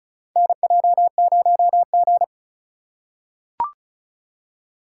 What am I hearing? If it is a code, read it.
Morse code, heard as a keyed tone, about 36 wpm, 700 Hz. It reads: NJ0G
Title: USA Call Signs - 4 Characters - 4x Character Spacing - 35wpm